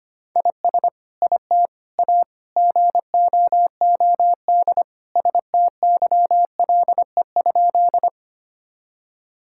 Morse code, heard as a keyed tone, about 25 wpm, 700 Hz. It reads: IS IT A GOOD STYLE?
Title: Sentences from Top 1000 Words - Review Part 5 - 25wpm